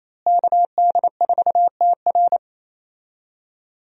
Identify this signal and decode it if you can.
Morse code, heard as a keyed tone, about 27 wpm, 700 Hz. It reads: KD4TR
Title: USA Call Signs - 5 Characters - Repeated 3x - 28wpm